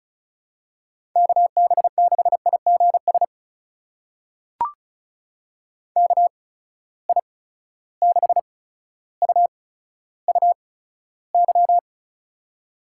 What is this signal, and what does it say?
Morse code, heard as a keyed tone, about 36 wpm, spaced as for 15 wpm, 700 Hz. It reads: KB6IGS KI6UUY
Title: USA Call Signs - 6 Characters - 8x Character Spacing - 35wpm